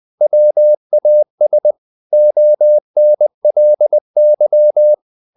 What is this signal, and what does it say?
Morse code, heard as a keyed tone, about 20 wpm, 600 Hz. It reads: WAS ONLY
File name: the old man and the sea part 2 20wpm